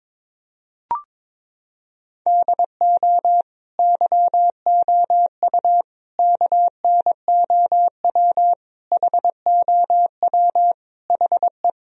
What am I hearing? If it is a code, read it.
Morse code, heard as a keyed tone, about 21 wpm, 700 Hz. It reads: DO YOU KNOW HOW HE
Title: Sentences from Top 1000 Words - Review Part 1 - 22wpm